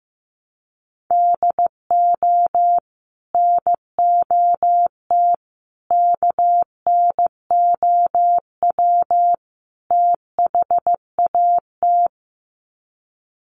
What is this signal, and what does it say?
Morse code, heard as a keyed tone, about 15 wpm, 700 Hz. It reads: DO NOT KNOW THAT
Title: Sets of 4 Words 15wpm